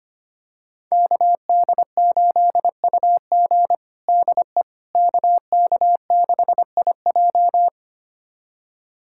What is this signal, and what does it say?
Morse code, heard as a keyed tone, about 25 wpm, 700 Hz. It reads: KD8UG DE KK6IJ
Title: USA Callsign DE Callsign - 5 Characters - 25wpm